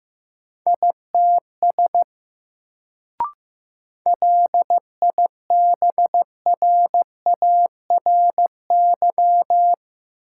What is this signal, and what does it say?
Morse code, heard as a keyed tone, about 15 wpm, 700 Hz. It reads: ITS LIBRARY
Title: Sentences from Top 600 Words - New Word Drill - 15wpm